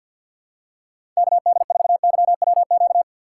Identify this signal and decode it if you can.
Morse code, heard as a keyed tone, about 53 wpm, 700 Hz. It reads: KD4YWQ
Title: USA Call Signs - 6 Characters - 6x Character Spacing - 50wpm